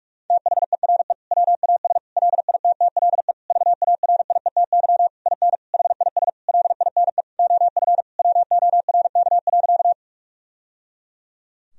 Morse code, heard as a keyed tone, about 46 wpm, 700 Hz. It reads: THERE WAS LITTLE VARIETY IN HIS LINE OF WORK.
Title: Sentences from Top 1000 Words - Review Part 4 - 45wpm